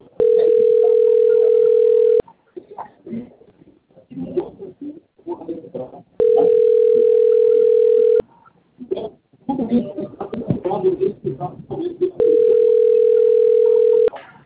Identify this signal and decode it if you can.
Ringback tone (North America), 440 + 480 Hz, with speech.
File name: out- - - - - .93795